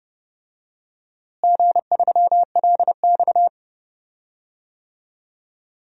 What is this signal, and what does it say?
Morse code, heard as a keyed tone, about 30 wpm, 700 Hz. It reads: G3LX
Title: DXCC-100 Call Signs - 4 Characters - Repeated 3x - 30wpm